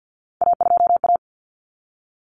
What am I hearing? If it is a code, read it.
Morse code, heard as a keyed tone, about 48 wpm, 700 Hz. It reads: A2A
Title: Common Cut Numbers - 3-Digit 50wpm